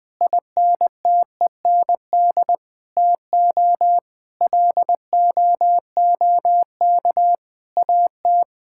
Morse code, heard as a keyed tone, about 21 wpm, 700 Hz. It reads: INTEND TO LOOK AT
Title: Sentences from Top 1600 Words 20wpm